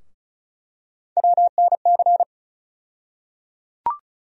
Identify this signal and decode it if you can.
Morse code, heard as a keyed tone, about 36 wpm, 700 Hz. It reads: WNC
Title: USA Call Sign Postfixes - 3 Letters 35wpm